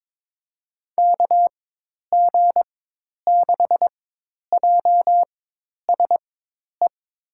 Morse code, heard as a keyed tone, about 21 wpm, spaced as for 10 wpm, 700 Hz. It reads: KG6JSE
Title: USA Call Signs - 6 Characters - 4x Character Spacing - 22wpm